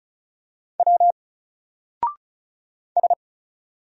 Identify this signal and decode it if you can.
Morse code, heard as a keyed tone, about 37 wpm, 700 Hz. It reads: W S
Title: Single Letter-Number 35wpm